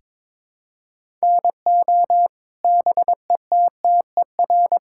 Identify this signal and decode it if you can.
Morse code, heard as a keyed tone, about 22 wpm, 700 Hz. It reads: NO BETTER
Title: Sentences from Top 1000 Words - Review Part 5 - 22wpm